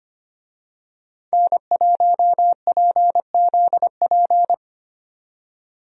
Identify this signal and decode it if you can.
Morse code, heard as a keyed tone, about 25 wpm, 700 Hz. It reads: N1PZP
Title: USA Call Signs - 5 Characters - 25wpm